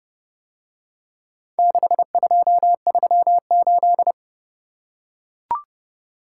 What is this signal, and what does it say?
Morse code, heard as a keyed tone, about 32 wpm, 700 Hz. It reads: 6238